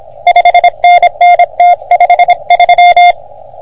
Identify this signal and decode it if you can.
Morse code, heard as a keyed tone, about 26 wpm, 720 Hz. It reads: 5NN T53
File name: OM3PA PILUP CUC 388